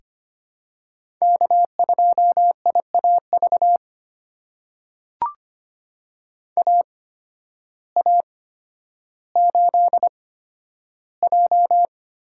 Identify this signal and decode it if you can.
Morse code, heard as a keyed tone, about 25 wpm, spaced as for 12 wpm, 700 Hz. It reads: K2IAV AA8J
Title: USA Call Signs - 5 Characters - 8x Character Spacing - 25wpm